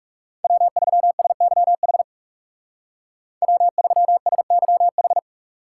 Morse code, heard as a keyed tone, about 35 wpm, 700 Hz. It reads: W3SYH W3SYH
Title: Speed Racing - USA 1x3 Call Signs 30wpm